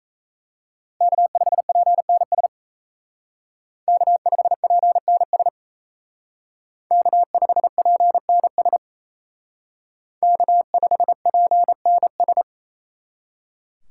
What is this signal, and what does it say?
Morse code, heard as a keyed tone, about 36 wpm, 700 Hz. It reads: K5PNS K5PNS K5PNS K5PNS